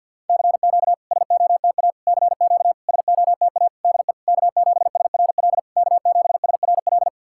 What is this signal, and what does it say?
Morse code, heard as a keyed tone, about 47 wpm, 690 Hz. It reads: CQ SOTA CQ SOTA DE K7SRL K7SRL
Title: SOTA QSO Practice 50wpm